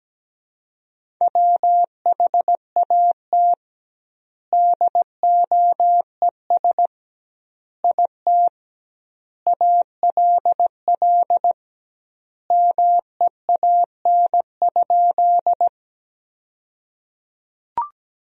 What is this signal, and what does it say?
Morse code, heard as a keyed tone, about 17 wpm, 700 Hz. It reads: WHAT DOES IT ALL MEAN?
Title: 2x Word-Spacing - Sentences from Top 200 Words - 17wpm